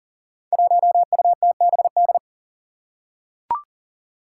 Morse code, heard as a keyed tone, about 39 wpm, 700 Hz. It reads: 1UTBD
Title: Five Challenging Letter-Number Combinations 40wpm